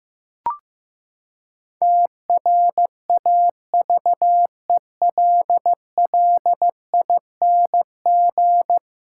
Morse code, heard as a keyed tone, about 16 wpm, 700 Hz. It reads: TRAVELLING
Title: Pangram Challenge v2 - 15wpm